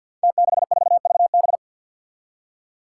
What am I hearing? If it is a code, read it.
Morse code, heard as a keyed tone, about 52 wpm, 700 Hz. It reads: T64VB
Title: Five Challenging Letter-Number Combinations 50wpm